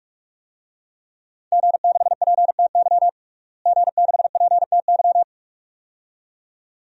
Morse code, heard as a keyed tone, about 46 wpm, 700 Hz. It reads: G6PTY G6PTY
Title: DXCC-100 Call Signs - 5 Characters - Repeated 2x - 45wpm